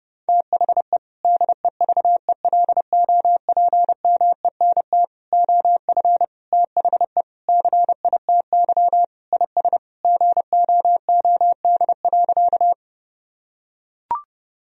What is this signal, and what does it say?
Morse code, heard as a keyed tone, about 32 wpm, 700 Hz. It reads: THE DEVELOPMENT OF THE CITY IS GOOD.
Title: Sentences from Top 1000 Words - Review Part 4 - 30wpm